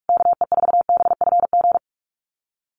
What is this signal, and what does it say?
Morse code, heard as a keyed tone, about 46 wpm, 690 Hz. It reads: KE4BFG
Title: USA Call Signs Repeated 3x 45wpm